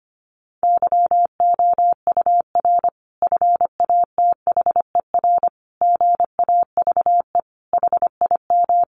Morse code, heard as a keyed tone, about 25 wpm, 700 Hz. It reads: YOUR FATHER GAVE HIM
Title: Sentences from Top 1400 Words 25wpm